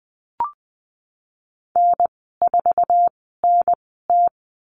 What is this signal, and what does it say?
Morse code, heard as a keyed tone, about 20 wpm, spaced as for 15 wpm, 700 Hz. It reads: N4NT
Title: 055 Lesson 10g - Letter r - Call Signs 20wpm